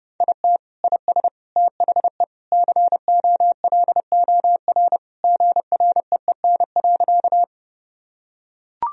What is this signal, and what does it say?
Morse code, heard as a keyed tone, about 31 wpm, 700 Hz. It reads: IT IS THE COLOR GREEN.
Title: Sentences from Top 1000 Words - Review Part 2 - 30wpm